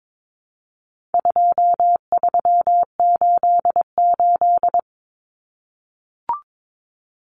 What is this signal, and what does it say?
Morse code, heard as a keyed tone, about 22 wpm, 700 Hz. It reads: 2388
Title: Numbers - 4-Digit 22wpm